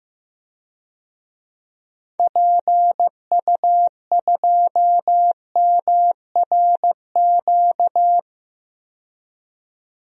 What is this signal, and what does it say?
Morse code, heard as a keyed tone, about 15 wpm, 700 Hz. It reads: PU2MRQ
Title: DXCC-100 Call Signs - 6 Characters - 15wpm